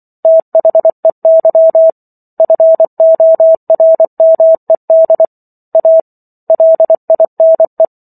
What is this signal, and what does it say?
Morse code, heard as a keyed tone, about 24 wpm, 650 Hz. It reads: THEY FORMED A LINE